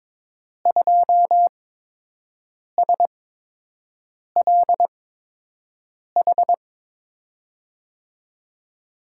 Morse code, heard as a keyed tone, about 22 wpm, spaced as for 6 wpm, 700 Hz. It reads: 2SLH
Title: USA Call Signs - 6 Characters - 8x Character Spacing - 22wpm